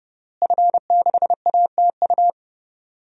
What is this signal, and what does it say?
Morse code, heard as a keyed tone, about 32 wpm, 700 Hz. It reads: F6ATU
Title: DXCC-100 Call Signs - 5 Characters - Repeated 3x - 30wpm